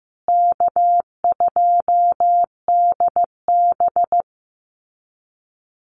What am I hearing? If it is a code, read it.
Morse code, heard as a keyed tone, about 15 wpm, 700 Hz. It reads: K2DB
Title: Easy Call Signs Repeated 3x - 15wpm